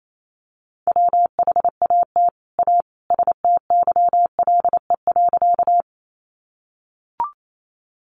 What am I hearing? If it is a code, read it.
Morse code, heard as a keyed tone, about 28 wpm, 700 Hz. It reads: WHAT A STYLE.
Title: Sentences from Top 800 Words 28wpm